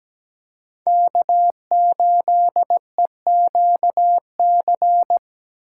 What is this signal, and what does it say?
Morse code, heard as a keyed tone, about 17 wpm, 700 Hz. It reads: K8EQC
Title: USA Call Signs - 5 Characters - 6x Character Spacing - 17wpm